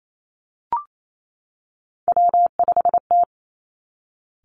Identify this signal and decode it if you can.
Morse code, heard as a keyed tone, about 28 wpm, 700 Hz. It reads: W5T
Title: Three Challenging Letter-Number Combinations 28wpm